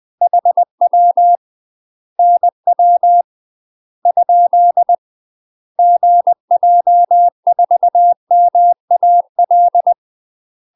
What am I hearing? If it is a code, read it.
Morse code, heard as a keyed tone, about 20 wpm, 700 Hz. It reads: HW NW ? GJ4MAL
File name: UK MASTER QSOs TRAINING/20WPM